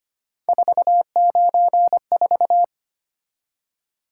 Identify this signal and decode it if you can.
Morse code, heard as a keyed tone, about 25 wpm, 700 Hz. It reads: 494
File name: Three Letter-Number Combinations - 25wpm